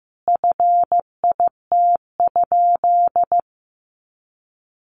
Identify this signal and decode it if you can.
Morse code, heard as a keyed tone, about 15 wpm, 700 Hz. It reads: FIT?
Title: Sentences from Top 800 Words 15wpm